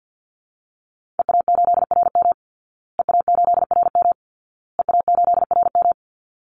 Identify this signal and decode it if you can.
Morse code, heard as a keyed tone, about 50 wpm, 700 Hz. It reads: EA8RM EA8RM EA8RM
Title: Call Signs Repeated 2x - 50wpm